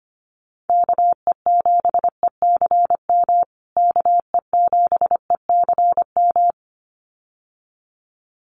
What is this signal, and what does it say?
Morse code, heard as a keyed tone, about 25 wpm, 700 Hz. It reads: KE7ECM KE7ECM
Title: USA Call Signs Repeated 2x 25wpm